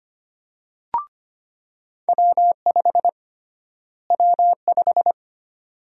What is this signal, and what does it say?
Morse code, heard as a keyed tone, about 25 wpm, 700 Hz. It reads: W5 W5
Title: USA Call Sign Prefixes - Repeated 2x - 25wpm